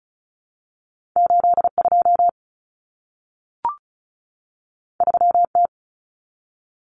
Morse code, heard as a keyed tone, about 36 wpm, 700 Hz. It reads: 82 3T
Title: Common Cut Numbers - 2-Digit 35wpm